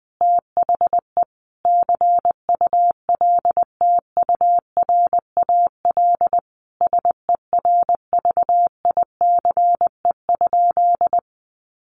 Morse code, heard as a keyed tone, about 20 wpm, 700 Hz. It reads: THE CULTURAL SERVICE?
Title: Sentences from Top 1500 Words 20wpm